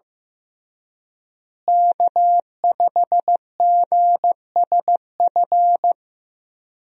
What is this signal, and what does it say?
Morse code, heard as a keyed tone, about 15 wpm, 700 Hz. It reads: K5GSF
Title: USA Call Signs - 5 Characters - Repeated 3x - 15wpm